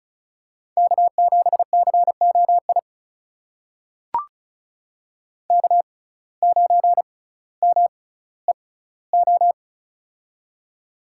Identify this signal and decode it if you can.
Morse code, heard as a keyed tone, about 36 wpm, spaced as for 18 wpm, 700 Hz. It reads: K7COI K9MEO
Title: USA Call Signs - 5 Characters - 6x Character Spacing - 35wpm